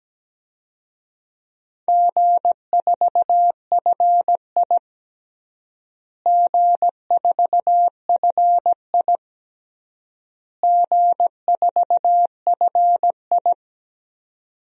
Morse code, heard as a keyed tone, about 17 wpm, 700 Hz. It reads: G4FI G4FI G4FI
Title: DXCC-100 Call Signs - 4 Characters - Repeated 3x - 17wpm